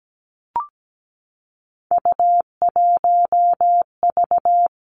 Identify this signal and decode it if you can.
Morse code, heard as a keyed tone, about 17 wpm, 700 Hz. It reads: U1V
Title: Three Challenging Letter-Number Combinations 17wpm